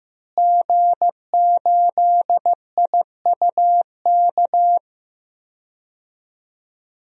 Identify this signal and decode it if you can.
Morse code, heard as a keyed tone, about 15 wpm, 700 Hz. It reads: G8IUK
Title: DXCC-100 Call Signs - 5 Characters - Repeated 3x - 15wpm